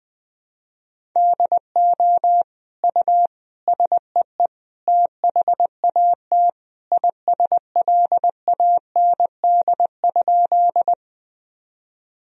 Morse code, heard as a keyed tone, about 21 wpm, 700 Hz. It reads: DO U SEE THAT ISLAND?
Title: Sentences from Top 500 Words and Abbreviations 20wpm